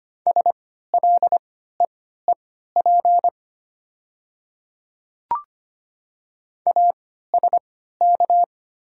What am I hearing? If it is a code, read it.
Morse code, heard as a keyed tone, about 25 wpm, spaced as for 14 wpm, 700 Hz. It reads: SLEEP ASK
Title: 3x Character Spacing - Top 500 Words - 25wpm